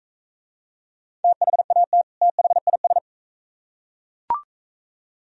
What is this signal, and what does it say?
Morse code, heard as a keyed tone, about 41 wpm, 700 Hz. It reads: THAT THIS